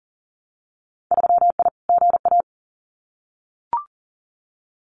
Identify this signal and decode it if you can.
Morse code, heard as a keyed tone, about 40 wpm, 700 Hz. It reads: 3I GA
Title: Winter Field Day Exchanges 40wpm